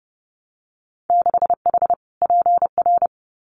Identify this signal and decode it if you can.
Morse code, heard as a keyed tone, about 32 wpm, 700 Hz. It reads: 6H PR
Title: Winter Field Day Exchanges Encore 30wpm